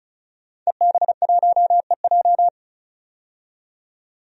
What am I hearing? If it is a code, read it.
Morse code, heard as a keyed tone, about 34 wpm, 700 Hz. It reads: EB1EJ
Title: DXCC-100 Call Signs - 5 Characters - 35wpm